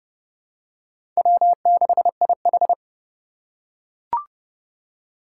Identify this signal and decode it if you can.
Morse code, heard as a keyed tone, about 30 wpm, 700 Hz. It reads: W6IH
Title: USA Call Signs - 4 Characters - 2x Character Spacing - 30wpm